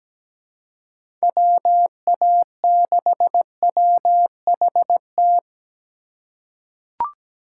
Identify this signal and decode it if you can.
Morse code, heard as a keyed tone, about 17 wpm, 700 Hz. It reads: WA6WHT
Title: USA Call Signs - 6 Characters - 4x Character Spacing - 17wpm